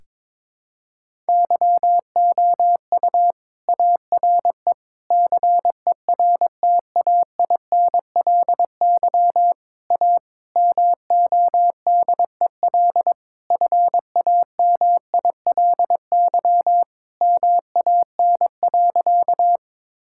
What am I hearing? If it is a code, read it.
Morse code, heard as a keyed tone, about 22 wpm, 700 Hz. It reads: YOU ARE CERTAINLY A MODEL FAMILY MAN.
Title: Sentences from Top 1000 Words - Review Part 5 - 22wpm